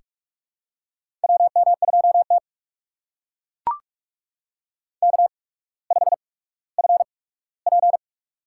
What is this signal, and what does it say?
Morse code, heard as a keyed tone, about 44 wpm, spaced as for 19 wpm, 700 Hz. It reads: WM2T K5FP
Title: USA Call Signs - 4 Characters - 8x Character Spacing - 45wpm